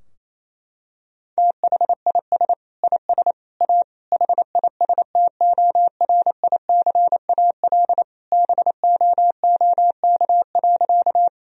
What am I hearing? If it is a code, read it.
Morse code, heard as a keyed tone, about 28 wpm, 700 Hz. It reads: THIS IS A HISTORICAL BOOK.
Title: Sentences from Top 1000 Words - Review Part 4 - 28wpm